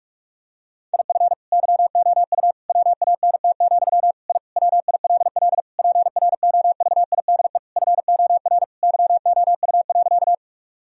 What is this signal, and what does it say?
Morse code, heard as a keyed tone, about 44 wpm, 700 Hz. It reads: IF YOU WANT, I WILL PROVIDE FOR YOU.
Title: Sentences from Top 1000 Words - Review Part 3 - 45wpm